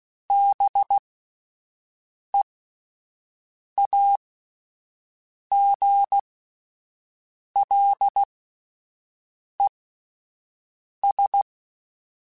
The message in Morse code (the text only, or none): BEAGLES